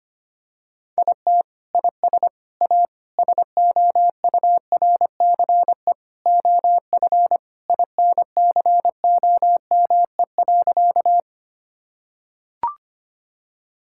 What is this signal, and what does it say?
Morse code, heard as a keyed tone, about 25 wpm, 700 Hz. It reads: IT IS A SOURCE OF INCOME.
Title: Sentences from Top 1000 Words - Review Part 5 - 25wpm